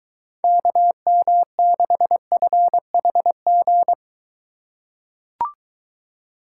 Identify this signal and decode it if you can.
Morse code, heard as a keyed tone, about 22 wpm, 700 Hz. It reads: KM6FHG